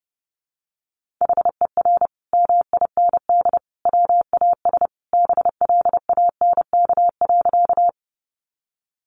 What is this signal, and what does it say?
Morse code, heard as a keyed tone, about 29 wpm, 700 Hz. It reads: HER MIND WAS BLANK.
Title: Sentences from Top 1500 Words 30wpm